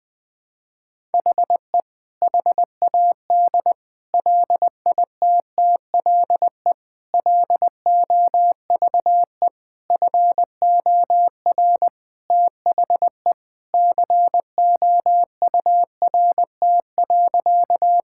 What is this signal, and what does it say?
Morse code, heard as a keyed tone, about 19 wpm, 700 Hz. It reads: HE HAD LITTLE LOVE FOR THE COURT.
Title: Sentences from Top 1600 Words 20wpm